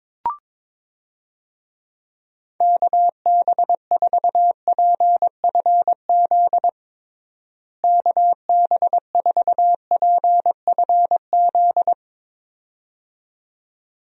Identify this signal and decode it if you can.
Morse code, heard as a keyed tone, about 22 wpm, 700 Hz. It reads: KB4PFZ KB4PFZ
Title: USA Call Signs - 6 Characters - Repeated 2x - 22wpm